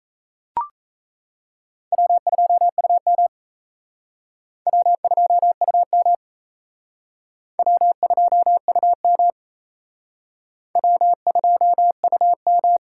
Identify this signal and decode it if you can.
Morse code, heard as a keyed tone, about 34 wpm, 700 Hz. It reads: W2UM W2UM W2UM W2UM
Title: Speed Racing - USA 1x2 Call Signs 28wpm